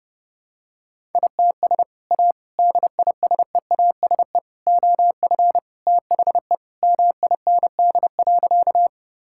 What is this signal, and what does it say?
Morse code, heard as a keyed tone, about 30 wpm, 700 Hz. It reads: ITS A DISEASE OF THE MIND.
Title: Sentences from Top 1000 Words - Review Part 6 - 30wpm